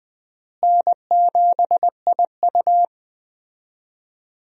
Morse code, heard as a keyed tone, about 20 wpm, 700 Hz. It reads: N7IU
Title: USA Call Signs - 4 Characters - Repeated 3x - 20wpm